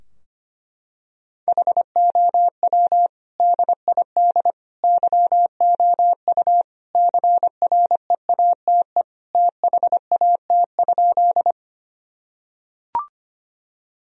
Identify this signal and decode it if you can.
Morse code, heard as a keyed tone, about 25 wpm, 700 Hz. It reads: HOW DID YOU CREATE THAT?
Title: Sentences from Top 1000 Words - Review Part 3 - 25wpm